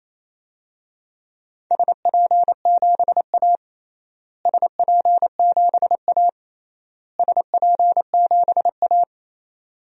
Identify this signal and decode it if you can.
Morse code, heard as a keyed tone, about 27 wpm, 700 Hz. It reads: SP7A SP7A SP7A
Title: DXCC-100 Call Signs - 4 Characters - Repeated 3x - 28wpm